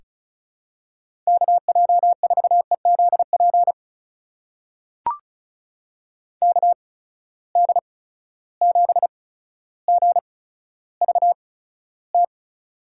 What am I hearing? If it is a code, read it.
Morse code, heard as a keyed tone, about 36 wpm, spaced as for 15 wpm, 700 Hz. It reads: KJ4EZP KD7GVT
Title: USA Call Signs - 6 Characters - 8x Character Spacing - 35wpm